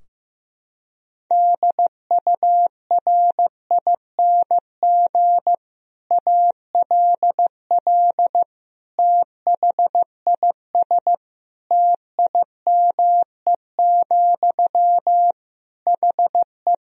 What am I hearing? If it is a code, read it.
Morse code, heard as a keyed tone, about 16 wpm, 700 Hz. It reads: DURING ALL THIS TIME, HE
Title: Sentences from Top 1000 Words - Review Part 3 - 15wpm